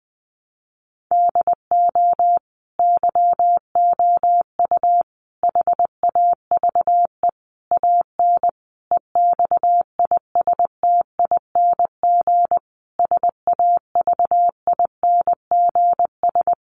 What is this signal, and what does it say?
Morse code, heard as a keyed tone, about 20 wpm, 700 Hz. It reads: DO YOU HAVE AN EXISTING SAVINGS
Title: Sentences from Top 1500 Words 20wpm